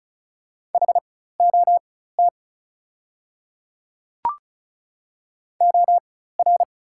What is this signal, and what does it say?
Morse code, heard as a keyed tone, about 36 wpm, spaced as for 16 wpm, 700 Hz. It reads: HOT OR
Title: 4x Character Spacing - Top 100 Words - 35wpm